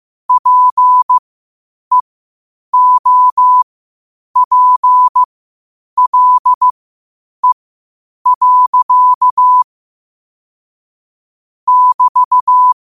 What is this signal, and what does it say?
Morse code, heard as a keyed tone, about 15 wpm, spaced as for 8 wpm, 1000 Hz. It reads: PEOPLE. <BT>
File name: qotdfiles / 8wpm